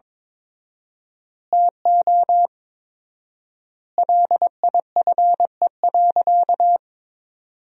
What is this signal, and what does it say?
Morse code, heard as a keyed tone, about 22 wpm, 700 Hz. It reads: TO LIFE.
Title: 4x Word-Spacing - Sentences from Top 400 Words - 22wpm